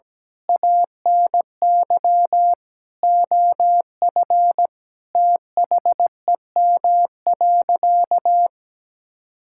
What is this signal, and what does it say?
Morse code, heard as a keyed tone, about 17 wpm, 700 Hz. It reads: ANY OF THEM.
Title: Sentences from Top 1000 Words - Review Part 6 - 17wpm